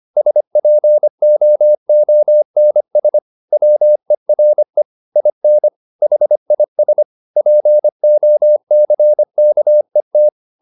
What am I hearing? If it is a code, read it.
Morse code, heard as a keyed tone, about 25 wpm, 600 Hz. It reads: SPOONS WERE IN HIS POCKET
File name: the old man and the sea part 1 25wpm